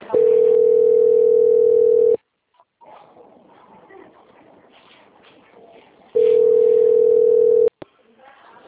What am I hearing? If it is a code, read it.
Ringback tone (North America), 440 + 480 Hz, with speech.
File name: out- - - - - .26025